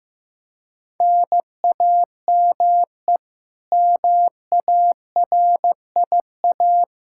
Morse code, heard as a keyed tone, about 15 wpm, 700 Hz. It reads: NAME MARIA
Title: Practice QSO - Exchange 1 - Beginning - 15wpm